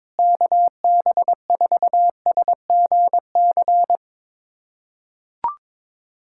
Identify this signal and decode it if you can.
Morse code, heard as a keyed tone, about 22 wpm, 700 Hz. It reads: KB4SGC
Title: USA Callsign DE Callsign - 6 Characters - 22wpm